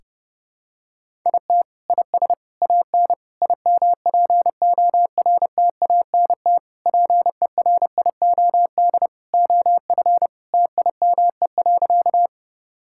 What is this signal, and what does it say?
Morse code, heard as a keyed tone, about 32 wpm, 700 Hz. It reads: IT IS AN IMPORTANT PERIOD OF TIME.
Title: Sentences from Top 1000 Words - Review Part 5 - 30wpm